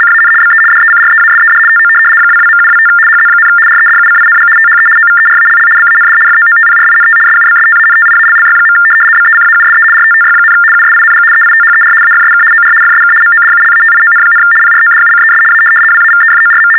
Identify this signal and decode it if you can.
RTTY, about 75 baud. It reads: CAN BE DECREASED TO 22.725 HZ, REDUCING THE OVERALL BAND FOOTPRINT SUBSTANTIALLY. BECAUSE RTTY, USING EITHER AFSK OR FSK MODULATION, PRODUCES A WAVEFORM
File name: teletype-RTTY-75